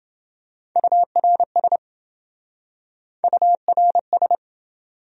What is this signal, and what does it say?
Morse code, heard as a keyed tone, about 27 wpm, 700 Hz. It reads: URS URS